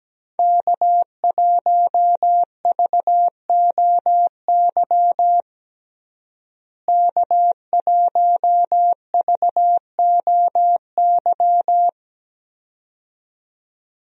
Morse code, heard as a keyed tone, about 17 wpm, 700 Hz. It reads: K1VOY K1VOY
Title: USA Call Signs - 5 Characters - Repeated 2x - 17wpm